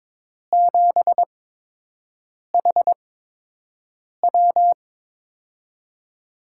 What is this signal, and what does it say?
Morse code, heard as a keyed tone, about 22 wpm, spaced as for 6 wpm, 700 Hz. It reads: 7HW
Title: USA Call Signs - 5 Characters - 8x Character Spacing - 22wpm